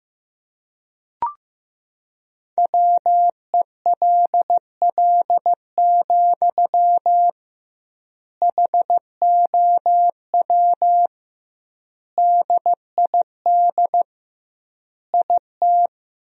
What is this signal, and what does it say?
Morse code, heard as a keyed tone, about 15 wpm, 700 Hz. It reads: WELL, HOW DID IT
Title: 2x Word-Spacing - Sentences from Top 200 Words - 15wpm